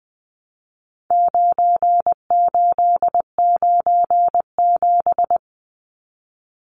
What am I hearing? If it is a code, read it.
Morse code, heard as a keyed tone, about 20 wpm, 700 Hz. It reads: 9897
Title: Numbers - 4-Digit 20wpm